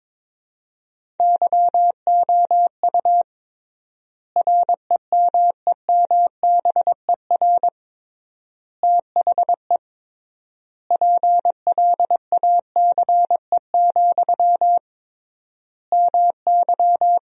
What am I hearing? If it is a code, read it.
Morse code, heard as a keyed tone, about 22 wpm, 700 Hz. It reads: YOU REMEMBER THE PLACE, MY
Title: 3x Word-Spacing - Sentences from Top 400 Words - 22wpm